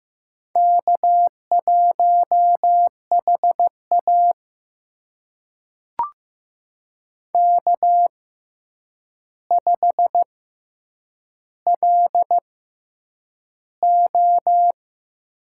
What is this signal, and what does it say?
Morse code, heard as a keyed tone, about 15 wpm, spaced as for 8 wpm, 700 Hz. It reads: K1HA K5LO
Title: USA Call Signs - 4 Characters - 6x Character Spacing - 15wpm